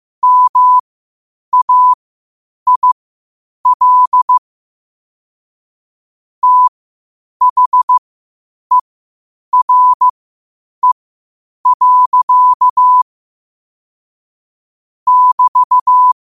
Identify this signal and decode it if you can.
Morse code, heard as a keyed tone, about 15 wpm, spaced as for 8 wpm, 1000 Hz. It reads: MAIL THERE. <BT>
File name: qotdfiles / 8wpm